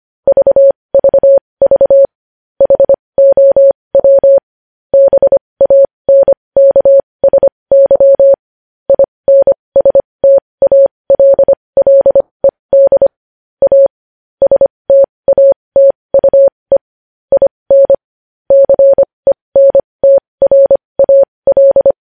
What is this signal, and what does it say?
Morse code, heard as a keyed tone, about 25 wpm, spaced as for 20 wpm, 550 Hz. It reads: VVV HOW BANKSY INSTALLED A STATUE IN CENTRAL
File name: podcast / News headlines-1x-25x20